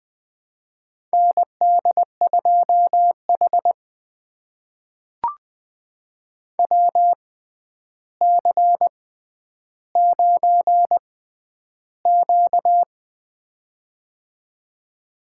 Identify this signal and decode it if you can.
Morse code, heard as a keyed tone, about 20 wpm, spaced as for 10 wpm, 700 Hz. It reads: ND2H WC9Q
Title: USA Call Signs - 4 Characters - 6x Character Spacing - 20wpm